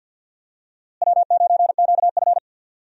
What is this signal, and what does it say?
Morse code, heard as a keyed tone, about 52 wpm, 700 Hz. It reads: W9QF
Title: USA Call Signs - 4 Characters - Repeated 3x - 50wpm